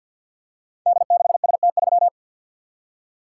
Morse code, heard as a keyed tone, about 50 wpm, 700 Hz. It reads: D6ST3
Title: Code Groups - 5 Letter:Numbers - 50wpm